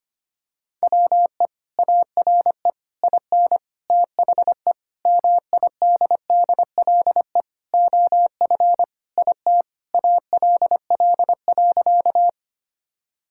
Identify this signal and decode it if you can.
Morse code, heard as a keyed tone, about 25 wpm, 700 Hz. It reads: WE ARE IN THE MIDDLE OF IT ALL.
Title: Sentences from Top 1000 Words - Review Part 6 - 25wpm